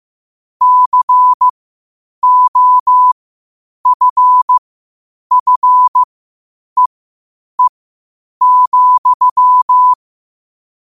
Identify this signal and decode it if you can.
Morse code, heard as a keyed tone, about 15 wpm, spaced as for 8 wpm, 1000 Hz. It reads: COFFEE,
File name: qotdfiles / 8wpm